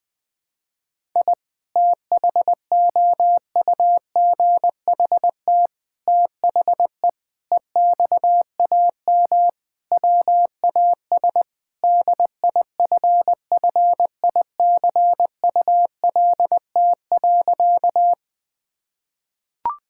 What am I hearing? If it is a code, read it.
Morse code, heard as a keyed tone, about 20 wpm, 700 Hz. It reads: I THOUGHT THE EXAM WAS DIFFICULT.
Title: Sentences from Top 1000 Words - Review Part 5 - 20wpm